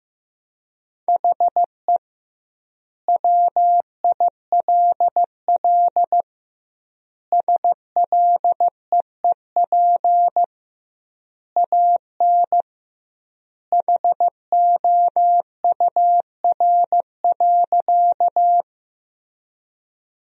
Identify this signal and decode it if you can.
Morse code, heard as a keyed tone, about 16 wpm, 700 Hz. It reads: HE WILL SLEEP AN HOUR.
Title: 2x Word-Spacing - Sentences from Top 500 Words - 15wpm